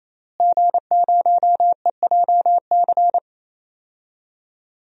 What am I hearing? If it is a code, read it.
Morse code, heard as a keyed tone, about 28 wpm, 700 Hz. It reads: G0EJC
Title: DXCC-100 Call Signs - 5 Characters - Repeated 3x - 28wpm